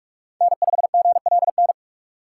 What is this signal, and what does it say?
Morse code, heard as a keyed tone, about 44 wpm, 700 Hz. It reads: N5GLN
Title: USA Call Signs - 5 Characters - 4x Character Spacing - 45wpm